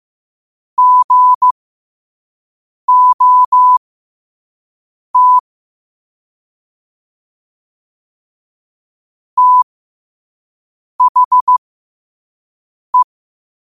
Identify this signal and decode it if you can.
Morse code, heard as a keyed tone, about 15 wpm, spaced as for 5 wpm, 1000 Hz. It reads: GOT THE
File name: qotdfiles / 5wpm